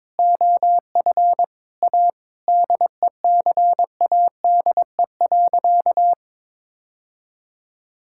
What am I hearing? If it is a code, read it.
Morse code, heard as a keyed tone, about 22 wpm, 700 Hz. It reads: OF A DECADE.
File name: Sentences with 7-Word Phrases - 22wpm